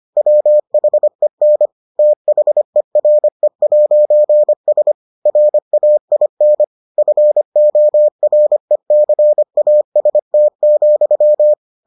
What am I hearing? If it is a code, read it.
Morse code, heard as a keyed tone, about 25 wpm, 600 Hz. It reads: WHEN THERE'S RAIN FORECAST,